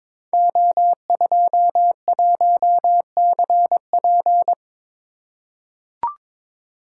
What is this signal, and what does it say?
Morse code, heard as a keyed tone, about 22 wpm, 700 Hz. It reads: O21CP
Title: Code Groups - 5 Letter:Numbers - 22wpm